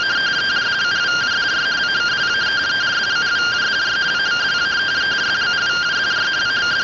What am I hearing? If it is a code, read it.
Two-tone teleprinter signal, about 170 Hz shift.